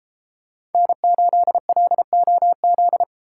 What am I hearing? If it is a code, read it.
Morse code, heard as a keyed tone, about 32 wpm, 700 Hz. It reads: N8LOZ